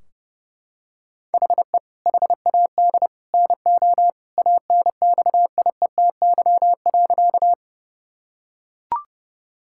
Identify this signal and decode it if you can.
Morse code, heard as a keyed tone, about 29 wpm, 700 Hz. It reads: HE HAD NO ANXIETY.
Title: Sentences from Top 1600 Words 30wpm